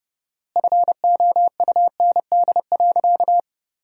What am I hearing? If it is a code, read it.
Morse code, heard as a keyed tone, about 32 wpm, 700 Hz. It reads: FOUND.
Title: 4x Word-Spacing - Sentences from Top 300 Words - 30wpm